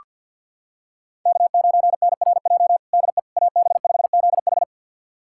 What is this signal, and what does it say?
Morse code, heard as a keyed tone, about 47 wpm, 700 Hz. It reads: K9NRJ DE AB5ZH
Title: USA Callsign DE Callsign - 5 Characters - 50wpm